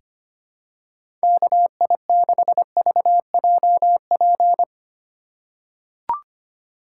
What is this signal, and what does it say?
Morse code, heard as a keyed tone, about 25 wpm, 700 Hz. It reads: KI6VJP
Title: USA Call Signs - 6 Characters - 3x Character Spacing - 25wpm